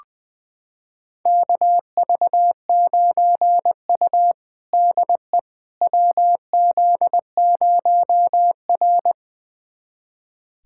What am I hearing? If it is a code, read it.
Morse code, heard as a keyed tone, about 20 wpm, 700 Hz. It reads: KV9U DE WZ0R
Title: USA Callsign DE Callsign - 4 Characters - 20wpm